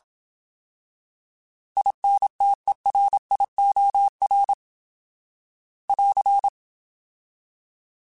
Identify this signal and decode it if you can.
Morse code, heard as a keyed tone, about 27 wpm, 800 Hz. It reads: INTERIOR <AR>
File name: Alphabetical / Alphabetical mp3s / Is @25